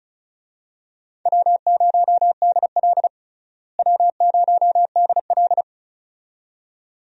Morse code, heard as a keyed tone, about 34 wpm, 700 Hz. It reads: W0DL W0DL
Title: USA Call Signs - 4 Characters - Repeated 2x - 35wpm